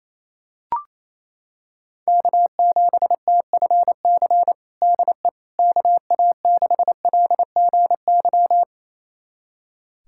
Morse code, heard as a keyed tone, about 28 wpm, 700 Hz. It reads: K7TFC DE KA6LGY
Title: USA Callsign DE Callsign 28wpm